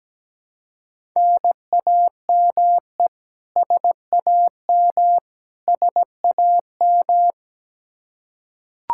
Morse code, heard as a keyed tone, about 17 wpm, 700 Hz. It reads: NAME SAM SAM
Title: Practice QSO - Exchange 1 - Beginning - 17wpm